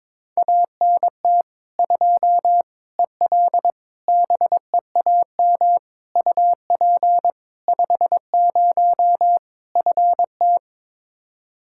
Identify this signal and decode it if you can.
Morse code, heard as a keyed tone, about 22 wpm, 700 Hz. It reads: ANT 2 EL BEAM UP 50 FT
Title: Practice QSO - Exchange 2 - Info - 22wpm